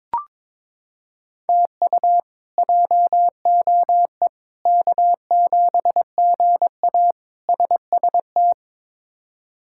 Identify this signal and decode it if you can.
Morse code, heard as a keyed tone, about 21 wpm, 700 Hz. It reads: TU JOE K7GA SST
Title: K1USN SST QSO Practice - 22wpm